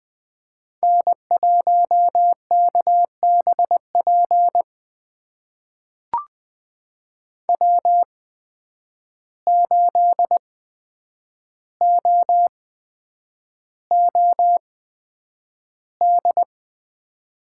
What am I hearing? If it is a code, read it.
Morse code, heard as a keyed tone, about 20 wpm, spaced as for 9 wpm, 700 Hz. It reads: N1KBP W8OOD
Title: USA Call Signs - 5 Characters - 8x Character Spacing - 20wpm